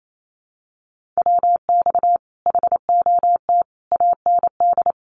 Morse code, heard as a keyed tone, about 28 wpm, 700 Hz. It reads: WX HOT AND
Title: Common QSO Phrases Encore 28wpm